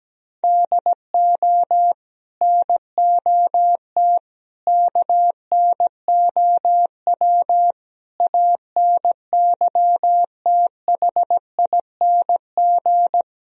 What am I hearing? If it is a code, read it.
Morse code, heard as a keyed tone, about 17 wpm, 700 Hz. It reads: DO NOT KNOW ANYTHING
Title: Speed-Racing - Sets of 5 Words 17wpm